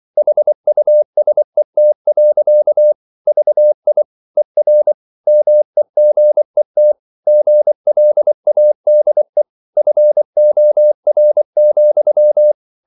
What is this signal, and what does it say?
Morse code, heard as a keyed tone, about 24 wpm, 600 Hz. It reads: HUSET. VI ER MEGET GLADE FOR,
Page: Ferie 23 wpm